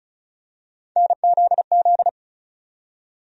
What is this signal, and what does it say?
Morse code, heard as a keyed tone, about 36 wpm, 700 Hz. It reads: NZZ
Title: USA Call Sign Postfixes - 3 Letters 35wpm